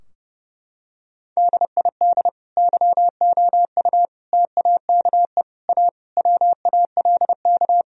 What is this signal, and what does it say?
Morse code, heard as a keyed tone, about 30 wpm, 700 Hz. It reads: DID YOU TAKE A WALK
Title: Sentences from Top 1000 Words - Review Part 4 - 30wpm